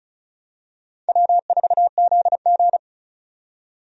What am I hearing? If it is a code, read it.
Morse code, heard as a keyed tone, about 36 wpm, 700 Hz. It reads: W4ZG
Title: USA Call Signs - 4 Characters - 35wpm